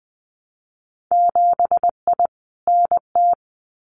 Morse code, heard as a keyed tone, about 20 wpm, 700 Hz. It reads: 7I NT
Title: Winter Field Day Exchanges Encore 20wpm